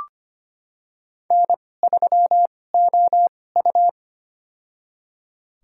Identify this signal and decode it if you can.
Morse code, heard as a keyed tone, about 25 wpm, spaced as for 18 wpm, 700 Hz. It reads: N3OU
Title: USA Call Signs - 4 Characters - 2x Character Spacing - 25wpm